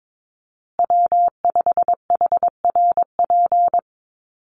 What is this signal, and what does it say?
Morse code, heard as a keyed tone, about 22 wpm, 700 Hz. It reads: W5HRP
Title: USA Call Signs Repeated 2x 22wpm